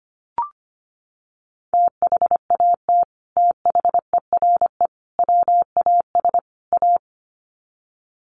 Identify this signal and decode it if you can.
Morse code, heard as a keyed tone, about 25 wpm, 700 Hz. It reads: THAT THERE WAS A
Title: Sets of 4 Words 25wpm